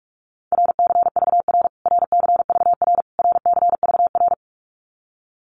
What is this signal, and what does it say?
Morse code, heard as a keyed tone, about 44 wpm, 700 Hz. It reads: RCVR RCVR RCVR
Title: Abbreviations Repeated 3x - 45wpm